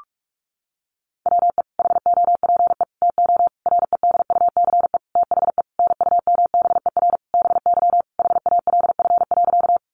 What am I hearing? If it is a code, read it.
Morse code, heard as a keyed tone, about 44 wpm, 700 Hz. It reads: WE HOPE TO REDUCE THE NUMBER BY HALF.
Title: Sentences from Top 1100 Words 45wpm